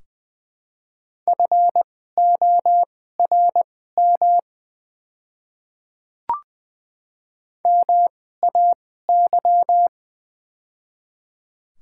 Morse code, heard as a keyed tone, about 20 wpm, spaced as for 15 wpm, 690 Hz. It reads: FORM MAY
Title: 2x Character Spacing - Top 300 Words - 20wpm